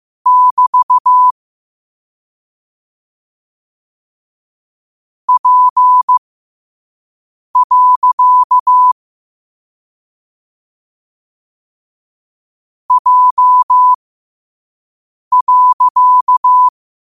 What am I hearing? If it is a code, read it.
Morse code, heard as a keyed tone, about 15 wpm, spaced as for 5 wpm, 1000 Hz. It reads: <BT> P. J.
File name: qotdfiles / 5wpm